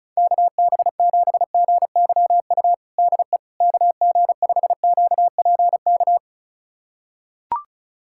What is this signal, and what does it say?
Morse code, heard as a keyed tone, about 36 wpm, 700 Hz. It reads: KB7GYU DE KG5QPK
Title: USA Callsign DE Callsign - 6 Characters - 35wpm